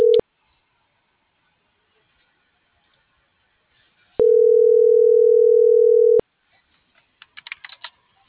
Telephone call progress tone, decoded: ringback tone

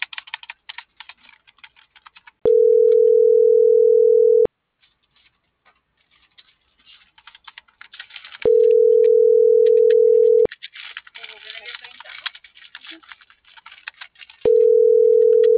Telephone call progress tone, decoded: ringback tone

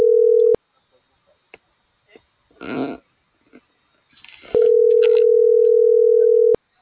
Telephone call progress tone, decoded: ringback tone

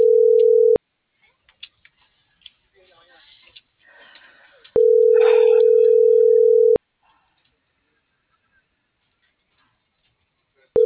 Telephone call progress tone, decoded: ringback tone

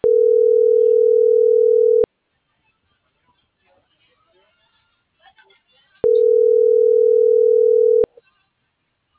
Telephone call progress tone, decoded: ringback tone

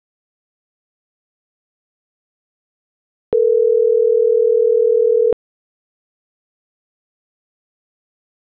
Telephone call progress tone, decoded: ringback tone

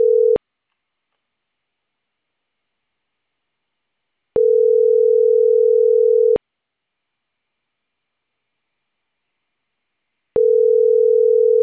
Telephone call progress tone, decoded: ringback tone

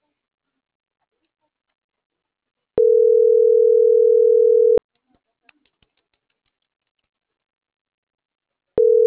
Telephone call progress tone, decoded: ringback tone

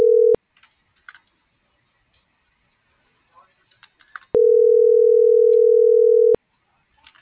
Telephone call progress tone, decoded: ringback tone

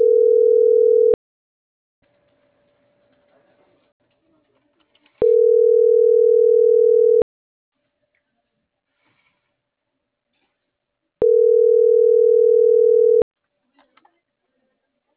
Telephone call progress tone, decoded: ringback tone

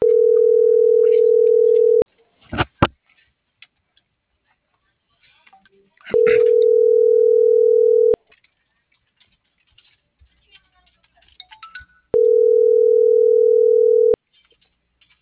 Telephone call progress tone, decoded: ringback tone

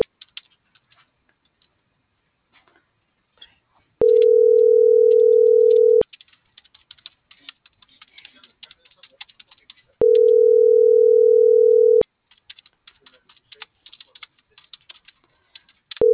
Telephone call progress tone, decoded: ringback tone

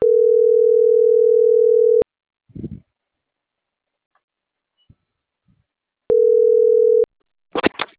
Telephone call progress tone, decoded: ringback tone

